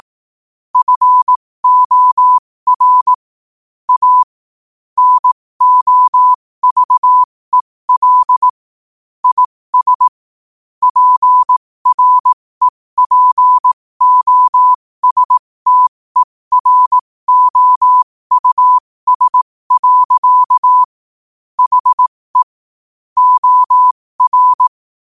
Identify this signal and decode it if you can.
Morse code, heard as a keyed tone, about 18 wpm, spaced as for 15 wpm, 1000 Hz. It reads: FOR A NOVEL IS PREPOSTEROUS. HE OR
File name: qotdfiles / 15wpm